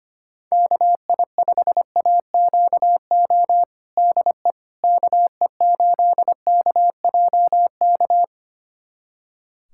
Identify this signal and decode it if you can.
Morse code, heard as a keyed tone, about 25 wpm, 700 Hz. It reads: KI5AQO DE KE8KJK
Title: USA Callsign DE Callsign 25wpm